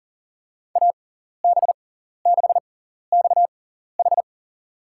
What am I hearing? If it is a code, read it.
Morse code, heard as a keyed tone, about 40 wpm, spaced as for 14 wpm, 700 Hz. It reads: AB6XH
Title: USA Call Signs - 5 Characters - 6x Character Spacing - 40wpm